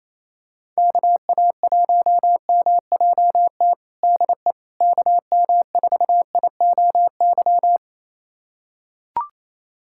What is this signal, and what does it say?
Morse code, heard as a keyed tone, about 28 wpm, 700 Hz. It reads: KA1MJT DE KM4IOY
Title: USA Callsign DE Callsign - 6 Characters - 28wpm